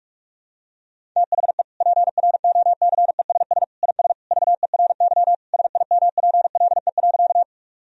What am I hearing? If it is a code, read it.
Morse code, heard as a keyed tone, about 46 wpm, 700 Hz. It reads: THE PROCESS IS VERY SIMPLE.
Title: Sentences from Top 1000 Words - Review Part 3 - 45wpm